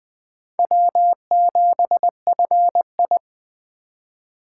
Morse code, heard as a keyed tone, about 23 wpm, 700 Hz. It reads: W7FI